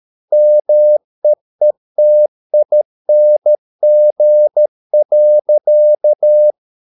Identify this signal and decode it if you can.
Morse code, heard as a keyed tone, about 13 wpm, 600 Hz. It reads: MEETING.